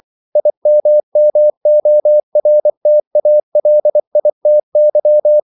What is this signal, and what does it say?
Morse code, heard as a keyed tone, about 24 wpm, 600 Hz. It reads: IMMORTALITY
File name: CW / Practice / EnglishQuotations0424